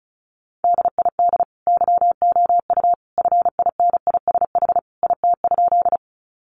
Morse code, heard as a keyed tone, about 34 wpm, 700 Hz. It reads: DID YOU FINISH IT?
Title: Sentences from Top 1200 Words 35wpm